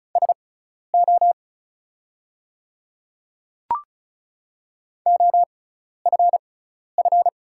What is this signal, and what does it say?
Morse code, heard as a keyed tone, about 36 wpm, spaced as for 12 wpm, 690 Hz. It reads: SO OFF
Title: 6x Character Spacing - Top 200 Words - 35wpm